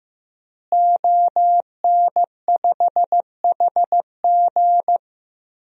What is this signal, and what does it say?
Morse code, heard as a keyed tone, about 15 wpm, 700 Hz. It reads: ON5HG
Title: DXCC-100 Call Signs - 5 Characters - Repeated 3x - 15wpm